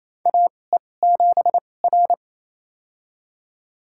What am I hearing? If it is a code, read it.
Morse code, heard as a keyed tone, about 28 wpm, spaced as for 20 wpm, 700 Hz. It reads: AE7R
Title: USA Call Signs - 4 Characters - 2x Character Spacing - 28wpm